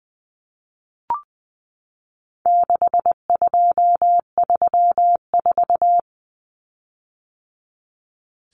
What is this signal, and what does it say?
Morse code, heard as a keyed tone, about 20 wpm, 700 Hz. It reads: 6234
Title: Common Cut Numbers - 4-Digit 20wpm